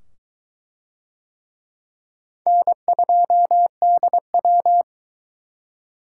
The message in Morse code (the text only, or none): N2DW